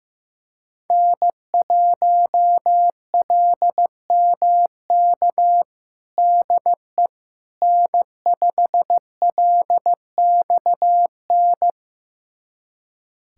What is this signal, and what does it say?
Morse code, heard as a keyed tone, about 16 wpm, 700 Hz. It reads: N1LMK DE N5LXN
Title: USA Callsign DE Callsign - 5 Characters - 15wpm